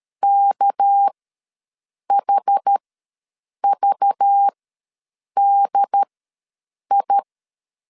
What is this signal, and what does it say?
Morse code, heard as a keyed tone, about 13 wpm, spaced as for 7 wpm, 800 Hz. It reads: KHVDI
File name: LEZIONI / 6 PKX0 / 6 PK ws 12wpm 3116